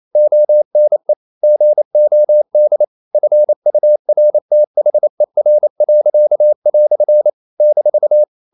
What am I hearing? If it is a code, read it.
Morse code, heard as a keyed tone, about 28 wpm, 600 Hz. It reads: ONE GOD FURTHER." -
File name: CW / Practice / EnglishQuotations0128